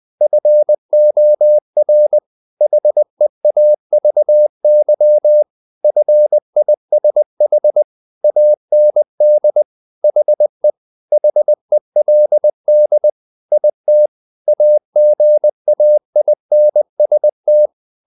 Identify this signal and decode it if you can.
Morse code, heard as a keyed tone, about 20 wpm, 600 Hz. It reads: FOR HEAVY FISH AND HE HELD IT AGAINST
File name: the old man and the sea / the old man and the sea part 1 20wpm